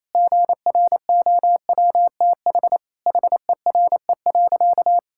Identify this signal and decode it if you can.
Morse code, heard as a keyed tone, about 28 wpm, 700 Hz. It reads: GROWTH HERE.
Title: Sentences from Top 1000 Words - Review Part 5 - 28wpm